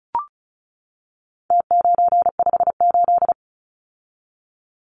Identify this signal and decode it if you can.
Morse code, heard as a keyed tone, about 34 wpm, 700 Hz. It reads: T958
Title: Numbers - Contest Style - 4-Digit 35wpm